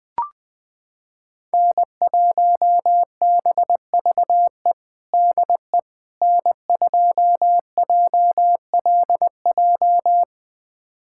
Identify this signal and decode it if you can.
Morse code, heard as a keyed tone, about 20 wpm, 700 Hz. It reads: N1BVE DE N2JLJ
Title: USA Callsign DE Callsign - 5 Characters - 20wpm